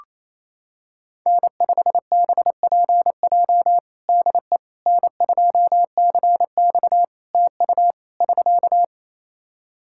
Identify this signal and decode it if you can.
Morse code, heard as a keyed tone, about 28 wpm, 700 Hz. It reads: N5BPJ DE N2CX TU <SK>
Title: K1USN SST QSO Practice 28wpm